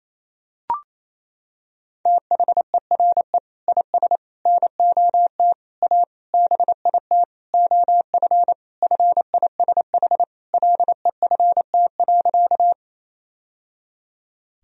Morse code, heard as a keyed tone, about 28 wpm, 700 Hz. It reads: THERE IS NOT A BIT OF FISH LEFT.
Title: Sentences from Top 1000 Words - Review Part 1 - 28wpm